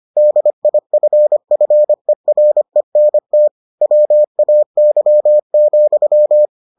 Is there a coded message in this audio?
Yes: Morse code from 0.2 to 6.5 s, heard as a keyed tone, about 25 wpm, 600 Hz. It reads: DIFFERENT WAY,